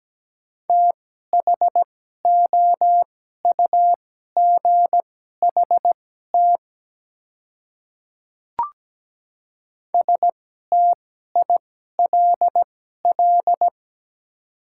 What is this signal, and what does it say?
Morse code, heard as a keyed tone, about 17 wpm, spaced as for 12 wpm, 700 Hz. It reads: THOUGHT STILL
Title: 2x Character Spacing - Top 300 Words - 17wpm